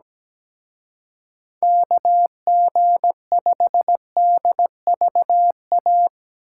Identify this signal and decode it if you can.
Morse code, heard as a keyed tone, about 18 wpm, 700 Hz. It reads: KG5DVA
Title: Speed Racing - USA 2x3 Call Signs 17wpm